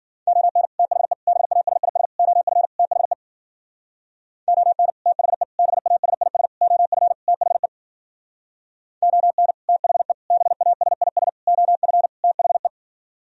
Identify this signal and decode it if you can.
Morse code, heard as a keyed tone, about 52 wpm, 700 Hz. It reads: ON THE BASIS OF THE ON THE BASIS OF THE ON THE BASIS OF THE